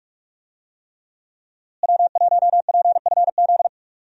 Morse code, heard as a keyed tone, about 46 wpm, 700 Hz. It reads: W1PFZ
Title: USA Call Signs - 5 Characters - Repeated 3x - 45wpm